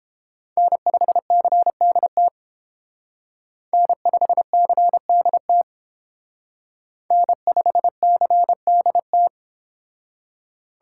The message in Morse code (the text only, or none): N5CDT N5CDT N5CDT